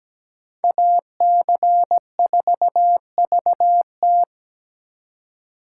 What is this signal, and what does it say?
Morse code, heard as a keyed tone, about 20 wpm, 700 Hz. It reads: AC4VT
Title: Speed Racing - USA 2x2 Call Signs 17wpm